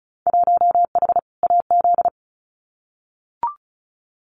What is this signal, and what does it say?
Morse code, heard as a keyed tone, about 36 wpm, 700 Hz. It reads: 1H AZ
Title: Winter Field Day Exchanges 35wpm